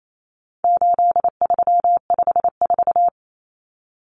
Morse code, heard as a keyed tone, about 28 wpm, 700 Hz. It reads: 8354
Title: Numbers - 4-Digit 28wpm